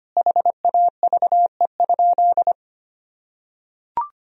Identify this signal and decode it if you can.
Morse code, heard as a keyed tone, about 25 wpm, 700 Hz. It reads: HAVE?
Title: Sentences from Top 1000 Words - Review Part 2 - 25wpm